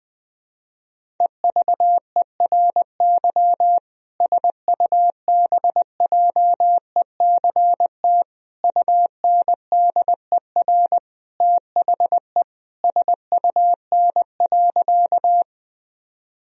Morse code, heard as a keyed tone, about 20 wpm, 700 Hz. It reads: EVERY SUBJECT UNDER THE SUN.
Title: Sentences from Top 1000 Words - Review Part 1 - 20wpm